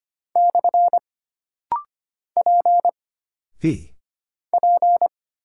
Morse code, heard as a keyed tone, about 25 wpm, 700 Hz. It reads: / P P
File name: Single Character (Letter:Number:Punctuation:Prosign) - Rapid-Fire - 25wpm